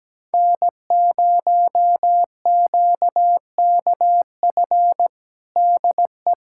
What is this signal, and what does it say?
Morse code, heard as a keyed tone, about 17 wpm, 700 Hz. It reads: N0QKF DE
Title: USA Callsign DE Callsign - 5 Characters - 17wpm